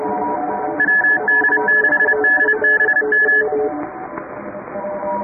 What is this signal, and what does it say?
Morse code, heard as a keyed tone, about 30 wpm, 1700 Hz, with noise. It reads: WD6DBM
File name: chatham / 2350-12CW